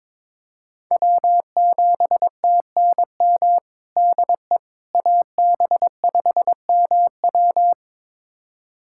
Morse code, heard as a keyed tone, about 22 wpm, 700 Hz. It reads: W7TNM DE AB5MW
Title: USA Callsign DE Callsign - 5 Characters - 22wpm